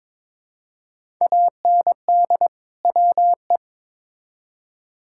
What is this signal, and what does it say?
Morse code, heard as a keyed tone, about 22 wpm, 700 Hz. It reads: AND WE
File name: Build A Phrase - 2-Word Phrases - 22wpm